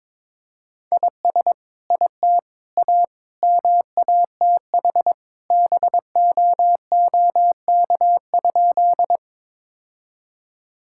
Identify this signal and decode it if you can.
Morse code, heard as a keyed tone, about 23 wpm, 700 Hz. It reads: IS IT A MATH BOOK?
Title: Sentences from Top 1000 Words - Review Part 6 - 22wpm